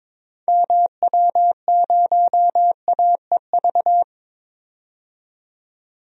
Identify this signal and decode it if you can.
Morse code, heard as a keyed tone, about 22 wpm, 700 Hz. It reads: MW0AEV
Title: DXCC-100 Call Signs - 6 Characters - 22wpm